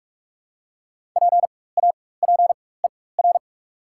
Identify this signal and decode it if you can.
Morse code, heard as a keyed tone, about 48 wpm, spaced as for 21 wpm, 690 Hz. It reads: PAPER
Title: 4x Character Spacing - Top 500 Words - 45wpm